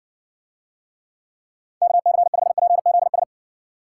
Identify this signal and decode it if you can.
Morse code, heard as a keyed tone, about 60 wpm, 700 Hz. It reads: KC5PZS